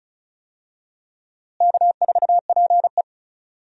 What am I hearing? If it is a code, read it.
Morse code, heard as a keyed tone, about 34 wpm, 700 Hz. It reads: K4PE
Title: USA Call Signs - 4 Characters - Repeated 3x - 35wpm